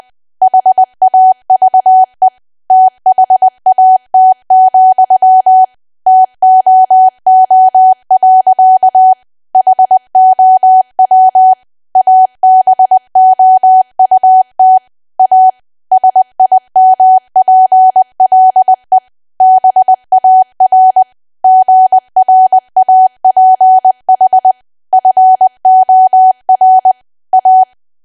Morse code, heard as a keyed tone, about 20 wpm, 750 Hz. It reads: HAVE THAT, TOO. HOW ABOUT A SIMPLE BAR GRAPH FOR A
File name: cwa / W1AW-Level3 / 20WPM